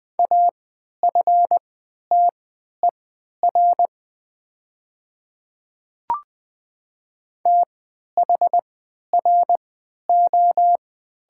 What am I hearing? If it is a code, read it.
Morse code, heard as a keyed tone, about 20 wpm, spaced as for 11 wpm, 700 Hz. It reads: AFTER THRO
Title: 3x Character Spacing - Top 200 Words - 20wpm